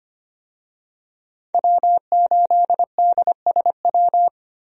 Morse code, heard as a keyed tone, about 25 wpm, 700 Hz. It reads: W8DSW
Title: USA Call Signs - 5 Characters - 25wpm